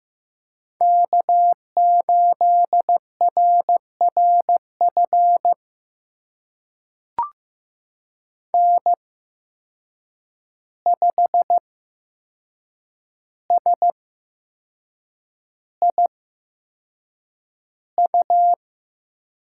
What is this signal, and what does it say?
Morse code, heard as a keyed tone, about 15 wpm, spaced as for 6 wpm, 700 Hz. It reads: K8RRF N5SIU
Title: USA Call Signs - 8x Character Spacing - 15wpm